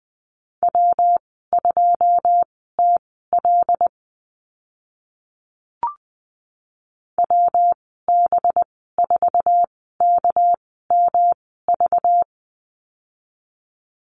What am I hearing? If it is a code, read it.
Morse code, heard as a keyed tone, about 20 wpm, spaced as for 14 wpm, 700 Hz. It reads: W2TL WB4KMV
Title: 213 Lesson 34g - Number 8 - Call Signs 20wpm